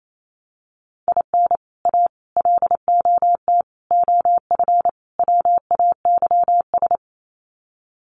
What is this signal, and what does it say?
Morse code, heard as a keyed tone, about 28 wpm, 700 Hz. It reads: IN A LOT OF WAYS
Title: Sets of 5 Words Encore 28wpm